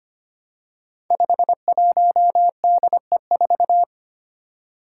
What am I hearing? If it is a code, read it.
Morse code, heard as a keyed tone, about 25 wpm, 700 Hz. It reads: 51DE4
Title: Five Challenging Letter-Number Combinations 25wpm